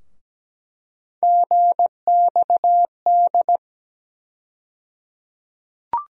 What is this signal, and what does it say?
Morse code, heard as a keyed tone, about 17 wpm, 700 Hz. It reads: GXD
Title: USA Call Sign Postfixes - 3 Letters 17wpm